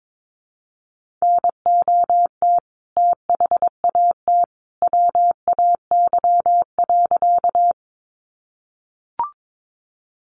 Morse code, heard as a keyed tone, about 22 wpm, 700 Hz. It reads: NOT THAT WAY.
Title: A Sound Of Thunder 22wpm